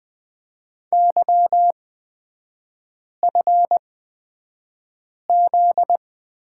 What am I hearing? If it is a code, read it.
Morse code, heard as a keyed tone, about 20 wpm, spaced as for 5 wpm, 700 Hz. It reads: YFZ
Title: 243 Lesson 39e - Letter z - Three Characters 20wpm